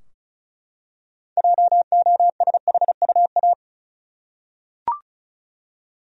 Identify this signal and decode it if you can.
Morse code, heard as a keyed tone, about 36 wpm, 700 Hz. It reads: JOSHUA
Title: USA Common Names - 35wpm